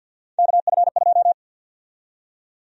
Morse code, heard as a keyed tone, about 53 wpm, 700 Hz. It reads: KF2
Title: USA Call Sign Prefixes - 50wpm